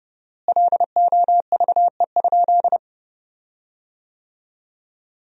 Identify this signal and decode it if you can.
Morse code, heard as a keyed tone, about 30 wpm, 700 Hz. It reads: LOVE?
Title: 4x Word-Spacing - Sentences from Top 400 Words - 30wpm